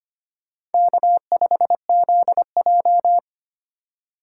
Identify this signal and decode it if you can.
Morse code, heard as a keyed tone, about 32 wpm, 700 Hz. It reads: K5ZJ